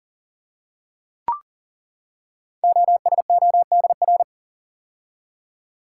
Morse code, heard as a keyed tone, about 39 wpm, 690 Hz. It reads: OSODR
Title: Code Groups - 5 Letters - 40wpm